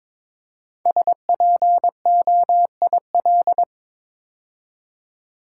Morse code, heard as a keyed tone, about 22 wpm, 700 Hz. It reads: SPOIL
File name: 5 Letter Words - First 3 Letters Given - 22wpm